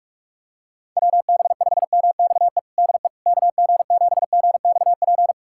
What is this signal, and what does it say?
Morse code, heard as a keyed tone, about 46 wpm, 700 Hz. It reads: WB5MXE DE KG7GXP
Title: USA Callsign DE Callsign - 6 Characters - 45wpm